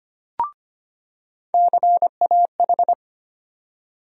Morse code, heard as a keyed tone, about 25 wpm, 700 Hz. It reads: CAH
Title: USA Call Sign Postfixes - 3 Letters 25wpm